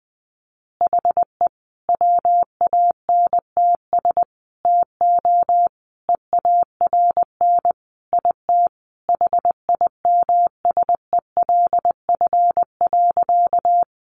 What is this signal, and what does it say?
Morse code, heard as a keyed tone, about 20 wpm, 700 Hz. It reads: HE WANTS TO EARN IT HIMSELF.
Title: Additional Easy Phrases 20wpm